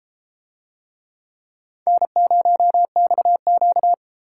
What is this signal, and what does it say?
Morse code, heard as a keyed tone, about 32 wpm, 700 Hz. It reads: N0XQ